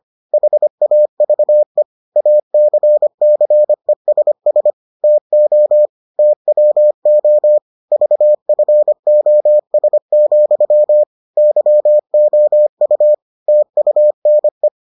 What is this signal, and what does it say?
Morse code, heard as a keyed tone, about 25 wpm, 600 Hz. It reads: HAVE ACCESS TO TWO VFOS, YOU TUNE